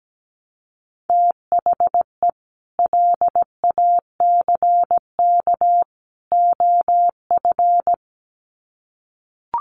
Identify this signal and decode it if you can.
Morse code, heard as a keyed tone, about 17 wpm, 700 Hz. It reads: THE LACK OF
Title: Sets of 3 Words 17wpm